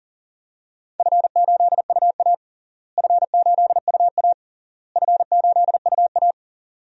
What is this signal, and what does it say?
Morse code, heard as a keyed tone, about 40 wpm, 700 Hz. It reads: F8UA F8UA F8UA
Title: DXCC-100 Call Signs - 4 Characters - Repeated 3x - 40wpm